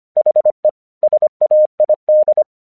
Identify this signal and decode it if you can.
Morse code, heard as a keyed tone, about 25 wpm, 610 Hz. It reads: HE SAID
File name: 1 English 25wpm / AQC16